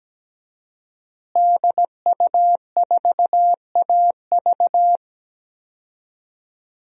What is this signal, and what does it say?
Morse code, heard as a keyed tone, about 17 wpm, 700 Hz. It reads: DU4AV
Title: DXCC-100 Call Signs - 5 Characters - Repeated 2x - 17wpm